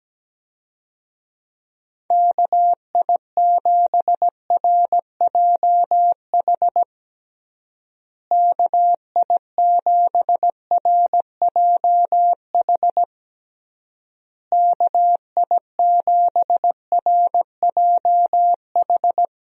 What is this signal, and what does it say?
Morse code, heard as a keyed tone, about 17 wpm, 700 Hz. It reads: KI7RJH KI7RJH KI7RJH
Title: USA Call Signs - 6 Characters - Repeated 3x - 17wpm